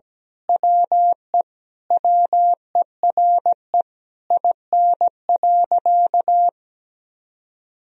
Morse code, heard as a keyed tone, about 17 wpm, 700 Hz. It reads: WE WERE IN.
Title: Sentences from Top 1000 Words - Review Part 3 - 17wpm